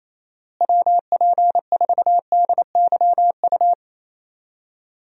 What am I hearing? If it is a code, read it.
Morse code, heard as a keyed tone, about 28 wpm, 700 Hz. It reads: WP4DYU
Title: USA Call Signs - 6 Characters - Repeated 2x - 28wpm